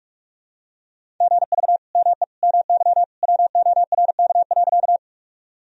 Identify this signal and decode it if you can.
Morse code, heard as a keyed tone, about 46 wpm, 700 Hz. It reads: GV ME MY WORK.
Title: Sentences from Top 200 Words and Abbreviations 45wpm